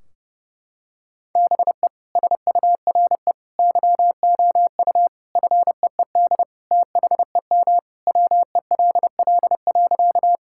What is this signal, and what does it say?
Morse code, heard as a keyed tone, about 30 wpm, 700 Hz. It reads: BE SURE YOU FEED THEM WELL.
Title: Sentences from Top 1600 Words 30wpm